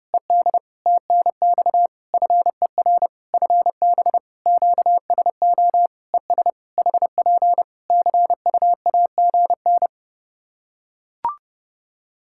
Morse code, heard as a keyed tone, about 30 wpm, 700 Hz. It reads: ED TNX FER FB QSO ES HP CUAGN
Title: Practice QSO - Exchange 4 - Ending - 30wpm